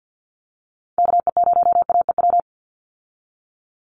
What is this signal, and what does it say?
Morse code, heard as a keyed tone, about 50 wpm, 700 Hz. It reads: KE0AEW
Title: USA Call Signs 50wpm